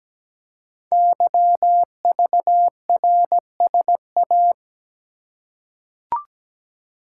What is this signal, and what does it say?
Morse code, heard as a keyed tone, about 17 wpm, 700 Hz. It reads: YVRSA
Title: Code Groups - 5 Letters - 17wpm